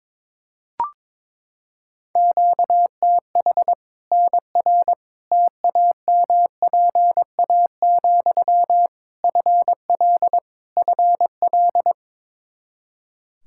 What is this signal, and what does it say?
Morse code, heard as a keyed tone, about 22 wpm, 700 Hz. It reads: QTH NR TAMPA, FL FL
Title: K1USN SST QSO Practice 22wpm